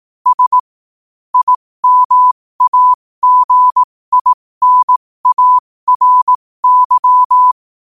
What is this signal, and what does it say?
Morse code, heard as a keyed tone, about 18 wpm, spaced as for 15 wpm, 1000 Hz. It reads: S IMAGINARY
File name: qotdfiles / 15wpm